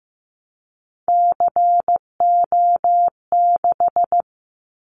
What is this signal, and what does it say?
Morse code, heard as a keyed tone, about 15 wpm, 700 Hz. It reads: CO6
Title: Callsign Prefixes for Top 100 DXCC - 15wpm